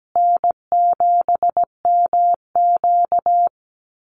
Morse code, heard as a keyed tone, about 17 wpm, 700 Hz. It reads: N7MQ
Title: Easy Call Signs Repeated 2x - 17wpm